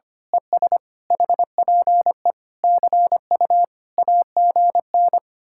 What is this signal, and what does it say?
Morse code, heard as a keyed tone, about 24 wpm, 700 Hz. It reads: ES HPE CU AGN
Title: Speed-Racing Common QSO Phrases Encore - 25wpm